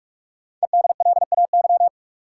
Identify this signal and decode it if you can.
Morse code, heard as a keyed tone, about 44 wpm, 700 Hz. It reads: EDLAY
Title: Code Groups - 5 Letters - 45wpm